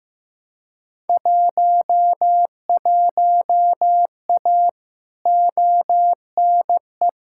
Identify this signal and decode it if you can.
Morse code, heard as a keyed tone, about 15 wpm, 700 Hz. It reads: 11A ONE
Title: Field Day Exchanges Encore - 15wpm